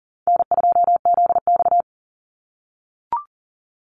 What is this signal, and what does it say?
Morse code, heard as a keyed tone, about 40 wpm, 700 Hz. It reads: N2ZX
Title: Call Signs - 40wpm